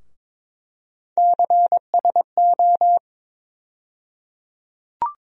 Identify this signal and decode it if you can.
Morse code, heard as a keyed tone, about 22 wpm, 700 Hz. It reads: CSO
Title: USA Call Sign Postfixes - 3 Letters 22wpm